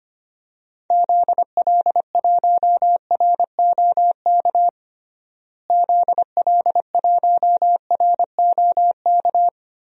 Morse code, heard as a keyed tone, about 25 wpm, 700 Hz. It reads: ZL1ROK ZL1ROK
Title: DXCC-100 Call Signs - 6 Characters - Repeated 3x - 25wpm